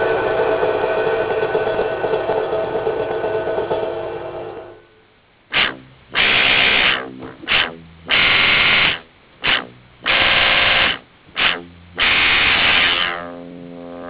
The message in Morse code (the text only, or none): none